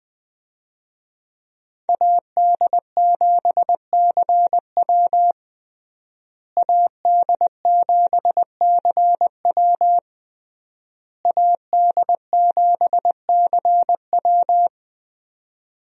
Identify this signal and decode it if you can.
Morse code, heard as a keyed tone, about 20 wpm, 700 Hz. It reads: AD7CW AD7CW AD7CW
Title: USA Call Signs - 5 Characters - Repeated 3x - 20wpm